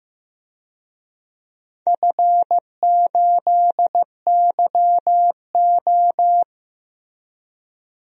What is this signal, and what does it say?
Morse code, heard as a keyed tone, about 15 wpm, 700 Hz. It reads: F8YO
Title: DXCC-100 Call Signs - 4 Characters - Repeated 3x - 15wpm